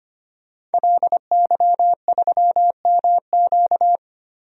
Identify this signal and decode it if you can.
Morse code, heard as a keyed tone, about 25 wpm, 700 Hz. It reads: LY3MQ
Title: DXCC-100 Call Signs - 5 Characters - 25wpm